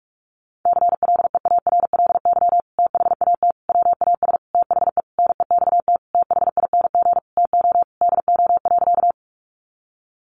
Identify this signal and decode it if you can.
Morse code, heard as a keyed tone, about 44 wpm, 700 Hz. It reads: CLEARLY THAT WAS THE NEXT THING TO DO.
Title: Sentences from Top 1300 Words 45wpm